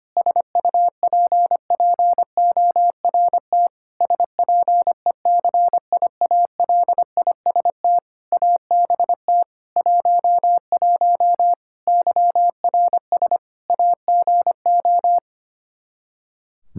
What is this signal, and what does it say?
Morse code, heard as a keyed tone, about 25 wpm, 700 Hz. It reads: SUPPORT SPECIALIST ABT 11 YRS AGO
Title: Practice QSO - Exchange 3 - More Info - 25wpm